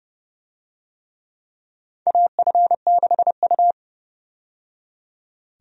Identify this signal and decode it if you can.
Morse code, heard as a keyed tone, about 32 wpm, 700 Hz. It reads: AF6U
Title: USA Call Signs - 4 Characters - 30wpm